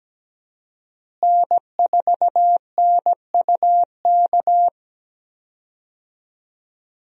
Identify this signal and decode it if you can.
Morse code, heard as a keyed tone, about 17 wpm, 700 Hz. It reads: N4NUK
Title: USA Call Signs - 5 Characters - Repeated 2x - 17wpm